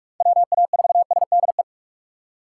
Morse code, heard as a keyed tone, about 46 wpm, 700 Hz. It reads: WA4SDE
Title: USA Call Signs - 6 Characters - 4x Character Spacing - 45wpm